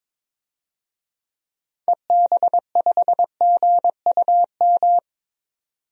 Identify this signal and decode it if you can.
Morse code, heard as a keyed tone, about 22 wpm, 700 Hz. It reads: EB5GUM
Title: DXCC-100 Call Signs - 6 Characters - 22wpm